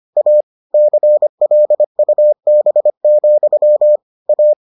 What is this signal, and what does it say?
Morse code, heard as a keyed tone, about 25 wpm, 600 Hz. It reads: A CLUB, A